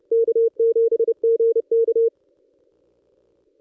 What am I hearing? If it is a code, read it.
Morse code, heard as a keyed tone, about 30 wpm, 460 Hz. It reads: K7GK
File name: contests / ss-cw-k5tr-op-n5rz -1600